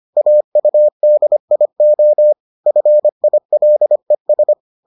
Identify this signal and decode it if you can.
Morse code, heard as a keyed tone, about 25 wpm, 600 Hz. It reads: AUDIO FILES